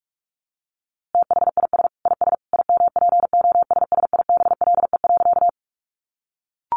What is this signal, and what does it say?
Morse code, heard as a keyed tone, about 44 wpm, 700 Hz. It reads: THIS IS IMPOSSIBLE.
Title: A Sound Of Thunder 45wpm